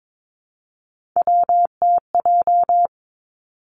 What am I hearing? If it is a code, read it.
Morse code, heard as a keyed tone, about 22 wpm, 700 Hz. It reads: WTJ
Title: Three Challenging Letter-Number Combinations 22wpm